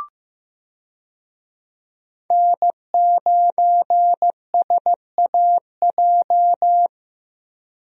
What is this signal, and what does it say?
Morse code, heard as a keyed tone, about 15 wpm, 700 Hz. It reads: N9SAJ
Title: USA Call Signs - 5 Characters - Repeated 3x - 15wpm